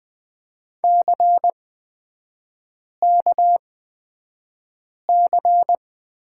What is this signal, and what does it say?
Morse code, heard as a keyed tone, about 20 wpm, spaced as for 5 wpm, 700 Hz. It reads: CKC
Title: 243 Lesson 39e - Letter z - Three Characters 20wpm